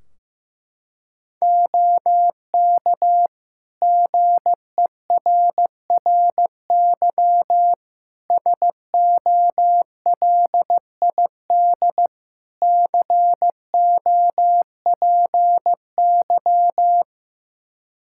Morse code, heard as a keyed tone, about 15 wpm, 700 Hz. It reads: OK GERRY SOLID COPY
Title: Practice QSO - Exchange 3 - More Info - 15wpm